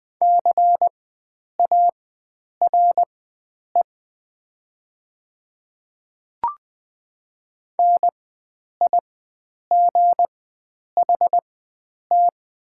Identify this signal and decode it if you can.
Morse code, heard as a keyed tone, about 20 wpm, spaced as for 9 wpm, 700 Hz. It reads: CARE NIGHT
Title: 4x Character Spacing - Top 300 Words - 20wpm